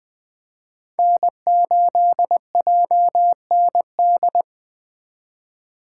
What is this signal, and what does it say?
Morse code, heard as a keyed tone, about 20 wpm, 700 Hz. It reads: N8JND
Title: USA Call Signs - 2x Character Spacing - 20wpm